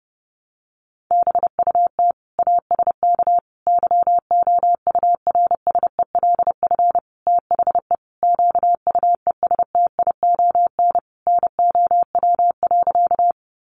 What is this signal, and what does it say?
Morse code, heard as a keyed tone, about 30 wpm, 700 Hz. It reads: BUT ASK YOURSELF THE QUESTION NOW.
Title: Sentences from Top 600 Words 30wpm